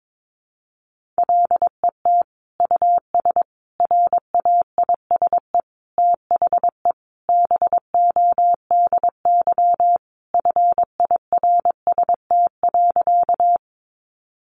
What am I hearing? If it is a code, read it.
Morse code, heard as a keyed tone, about 22 wpm, 700 Hz. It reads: LET US RAISE THE BODY FIRST.
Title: Sentences from Top 1400 Words 22wpm